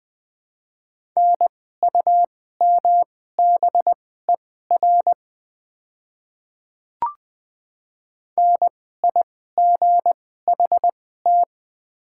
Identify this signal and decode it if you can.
Morse code, heard as a keyed tone, about 20 wpm, spaced as for 14 wpm, 700 Hz. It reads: NUMBER NIGHT
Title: 2x Character Spacing - Top 300 Words - 20wpm